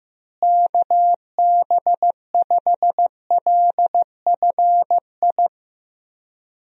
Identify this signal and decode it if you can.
Morse code, heard as a keyed tone, about 16 wpm, 700 Hz. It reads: KB5LFI
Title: USA Call Signs - 6 Characters - Repeated 3x - 15wpm